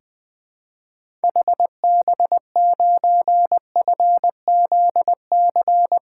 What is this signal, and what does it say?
Morse code, heard as a keyed tone, about 20 wpm, 700 Hz. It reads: HB9FZC
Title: 2023 SOTA Activator Call Signs - 20wpm